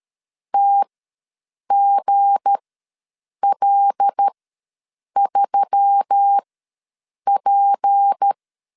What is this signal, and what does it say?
Morse code, heard as a keyed tone, about 13 wpm, spaced as for 7 wpm, 800 Hz. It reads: TGL3P
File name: LEZIONI / 7 GF9Z / 7 G ws 12wpm 3140